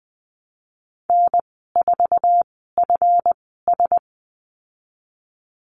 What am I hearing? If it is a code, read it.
Morse code, heard as a keyed tone, about 20 wpm, spaced as for 15 wpm, 700 Hz. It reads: N4FS
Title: 161 Lesson 26g - Letter g - Call Signs 20wpm